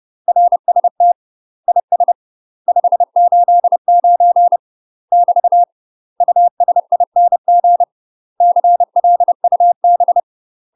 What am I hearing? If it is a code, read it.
Morse code, heard as a keyed tone, about 30 wpm, 700 Hz. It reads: RST IS 589 <BT> USING CLUB
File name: UK MASTER QSOs TRAINING / 30WPM / UK MASTER QSOs 0033